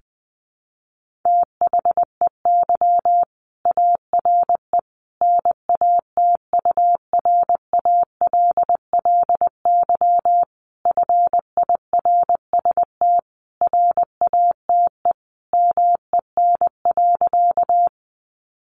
Morse code, heard as a keyed tone, about 20 wpm, 700 Hz. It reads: THEY ARE NATURALLY FIRST RATE MEN.
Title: Sentences from Top 1300 Words 20wpm